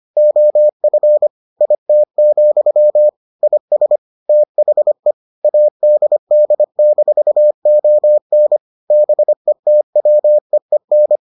Morse code, heard as a keyed tone, about 25 wpm, 600 Hz. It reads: OF IT, IS THE ADD-ON BETWEEN